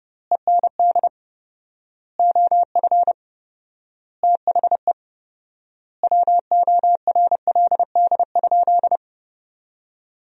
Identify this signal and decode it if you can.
Morse code, heard as a keyed tone, about 29 wpm, 700 Hz. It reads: END OF THE WORLD?
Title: 4x Word-Spacing - Sentences from Top 200 Words - 30wpm